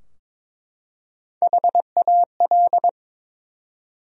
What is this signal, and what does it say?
Morse code, heard as a keyed tone, about 23 wpm, 700 Hz. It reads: HAL
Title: USA Common Names - Max 5 Letters - 22wpm